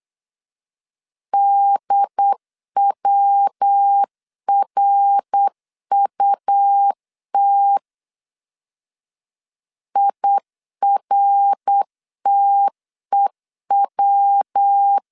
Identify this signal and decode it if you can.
Morse code, heard as a keyed tone, about 8 wpm, 800 Hz. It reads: DWRUT IRTEW
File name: CW / DET / 1 3 1500C S B 8wpm 4141